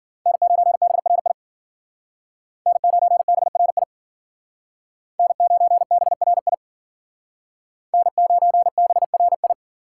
Morse code, heard as a keyed tone, about 50 wpm, 700 Hz. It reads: N9BRI N9BRI N9BRI N9BRI
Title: Speed Racing - USA Call Signs 40wpm